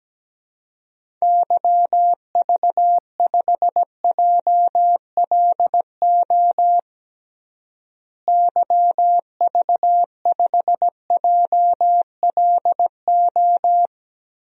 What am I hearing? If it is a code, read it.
Morse code, heard as a keyed tone, about 17 wpm, 700 Hz. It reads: YV5JLO YV5JLO
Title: DXCC-100 Call Signs - 6 Characters - Repeated 3x - 17wpm